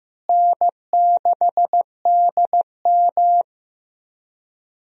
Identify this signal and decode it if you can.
Morse code, heard as a keyed tone, about 15 wpm, 700 Hz. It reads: N6DM
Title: USA Call Signs - 4 Characters - Repeated 3x - 15wpm